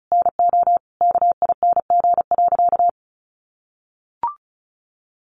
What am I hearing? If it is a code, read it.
Morse code, heard as a keyed tone, about 34 wpm, 700 Hz. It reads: NO KING.
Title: Sentences from Top 600 Words 35wpm